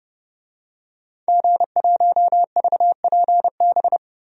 Morse code, heard as a keyed tone, about 30 wpm, 700 Hz. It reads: G1VPB
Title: DXCC-100 Call Signs - 5 Characters - Repeated 3x - 30wpm